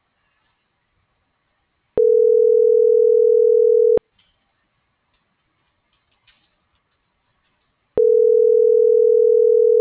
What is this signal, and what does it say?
Ringback tone (North America), 440 + 480 Hz.